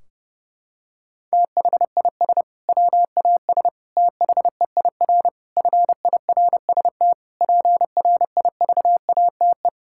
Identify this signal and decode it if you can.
Morse code, heard as a keyed tone, about 30 wpm, 700 Hz. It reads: THIS WAS THEIR FIRST PRIVATE
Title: Sentences from Top 1600 Words 30wpm